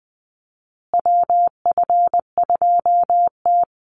Morse code, heard as a keyed tone, about 20 wpm, 700 Hz. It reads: WF2T
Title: USA Call Signs Repeated 2x 20wpm